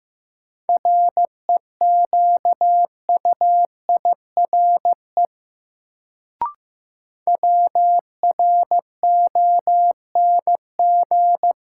Morse code, heard as a keyed tone, about 15 wpm, 700 Hz. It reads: REQUIRE WRONG
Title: Sentences from Top 900 Words - New Word Drill - 15wpm